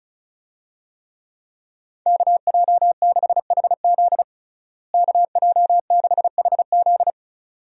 Morse code, heard as a keyed tone, about 36 wpm, 700 Hz. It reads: KJ6HZ KJ6HZ
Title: USA Call Signs - 5 Characters - Repeated 2x - 35wpm